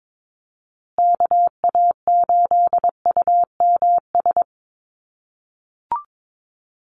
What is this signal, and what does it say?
Morse code, heard as a keyed tone, about 22 wpm, 700 Hz. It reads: KA8UMS
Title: USA Call Signs Repeated 2x 22wpm